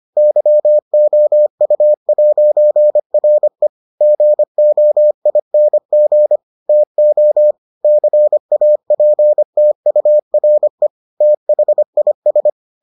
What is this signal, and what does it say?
Morse code, heard as a keyed tone, about 25 wpm, 600 Hz. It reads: YOU'RE GOING TO CAPTURE THIS